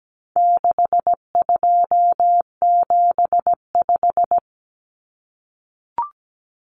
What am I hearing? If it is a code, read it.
Morse code, heard as a keyed tone, about 17 wpm, 700 Hz. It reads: 6275
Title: Numbers - 4-Digit 17wpm